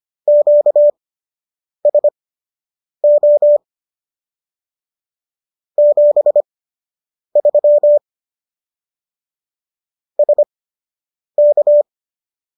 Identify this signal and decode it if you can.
Morse code, heard as a keyed tone, about 25 wpm, spaced as for 8 wpm, 590 Hz. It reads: QSO 73 SK